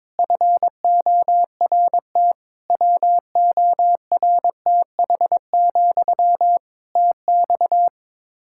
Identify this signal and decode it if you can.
Morse code, heard as a keyed tone, about 22 wpm, 700 Hz. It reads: FORT WORTH, TX
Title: Practice QSO - Exchange 1 - Beginning - 22wpm